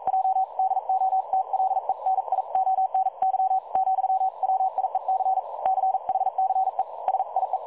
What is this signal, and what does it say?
Morse code, heard as a keyed tone, about 42 wpm, 760 Hz. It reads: J DOTLENIONY , WIEC CHCE SIE
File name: dlazaawansowanych / nagraniazpasma / websdr recording 3565.0kHz